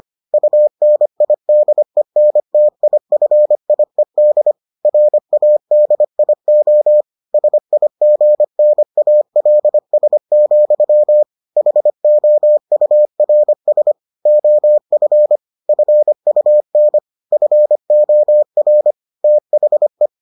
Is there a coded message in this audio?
Yes: Morse code from 0.3 to 20.1 s, heard as a keyed tone, about 25 wpm, 600 Hz. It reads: UNIDENTIFIED RADIO SIGNALS, HOURS OF FUN FOR THE